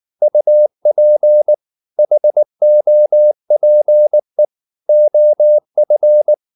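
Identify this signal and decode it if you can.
Morse code, heard as a keyed tone, about 19 wpm, 600 Hz. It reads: UP HOPE OF
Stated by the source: Morse code at 19 wpm from CW med Gnister, reading Ken Follett's novel Never in English.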